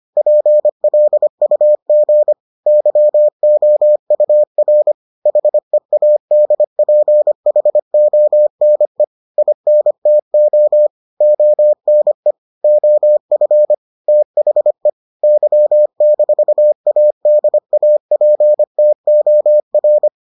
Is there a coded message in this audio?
Yes: Morse code from 0.2 to 20.1 s, heard as a keyed tone, about 25 wpm, 600 Hz. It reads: PLUG YOUR HEADPHONE INTO ONE OF THE Y-ADAPTOR